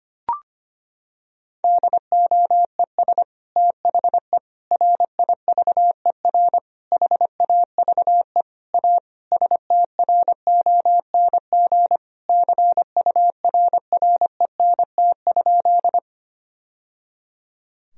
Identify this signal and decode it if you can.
Morse code, heard as a keyed tone, about 25 wpm, 700 Hz. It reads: DOES THE RIVER HAVE A STRONG CURRENT?
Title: Sentences from Top 1000 Words - Review Part 4 - 25wpm